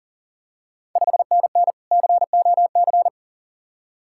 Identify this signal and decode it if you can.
Morse code, heard as a keyed tone, about 41 wpm, 690 Hz. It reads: 5NN COC
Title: Ontario QSO Party Exchange 40wpm